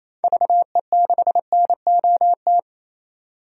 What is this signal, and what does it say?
Morse code, heard as a keyed tone, about 28 wpm, 700 Hz. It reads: VE6NOT
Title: DXCC-100 Call Signs - 6 Characters - 28wpm